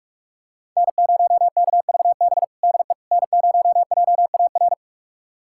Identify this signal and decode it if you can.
Morse code, heard as a keyed tone, about 44 wpm, 700 Hz. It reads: N0KVB DE N0JAR
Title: USA Callsign DE Callsign - 5 Characters - 45wpm